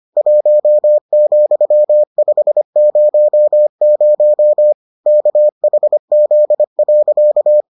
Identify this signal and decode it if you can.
Morse code, heard as a keyed tone, about 25 wpm, 600 Hz. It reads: 1,500 KHZ.